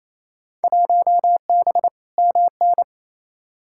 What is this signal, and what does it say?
Morse code, heard as a keyed tone, about 28 wpm, 700 Hz. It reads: 1B MN
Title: Field Day Exchanges Encore - 28wpm